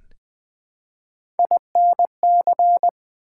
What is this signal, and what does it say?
Morse code, heard as a keyed tone, about 20 wpm, 700 Hz. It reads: INC
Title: Build A Word - 1 Letter At A Time - 5-Letter Words - 20wpm